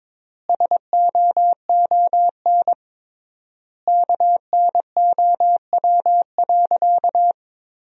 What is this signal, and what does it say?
Morse code, heard as a keyed tone, about 23 wpm, 700 Hz. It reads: SOON KNOW.
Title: 3x Word-Spacing - Sentences from Top 400 Words - 22wpm